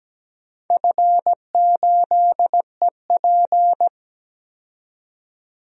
Morse code, heard as a keyed tone, about 17 wpm, 700 Hz. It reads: F8EP
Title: DXCC-100 Call Signs - 4 Characters - 17wpm